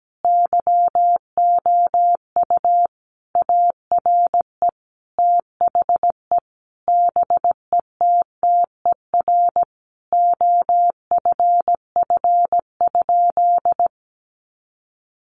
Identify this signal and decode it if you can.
Morse code, heard as a keyed tone, about 17 wpm, 700 Hz. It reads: YOU ARE THE BETTER OFF?
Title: Sentences from Top 500 Words 17wpm